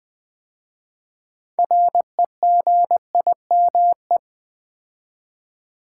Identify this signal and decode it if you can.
Morse code, heard as a keyed tone, about 20 wpm, 700 Hz. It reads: REGIME
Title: 6 Letter Words - First 3 Letters Given - 20wpm